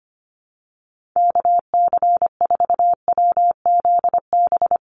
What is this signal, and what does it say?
Morse code, heard as a keyed tone, about 25 wpm, 700 Hz. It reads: KC4WZB
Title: NA SOTA Activator Callsigns 25wpm